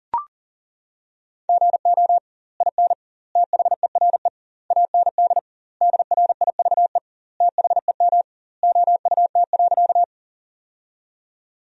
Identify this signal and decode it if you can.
Morse code, heard as a keyed tone, about 40 wpm, 700 Hz. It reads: GO IN THERE AND DRIVE THEM OUT.
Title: 2x Word-Spacing - Sentences from Top 500 Words - 40wpm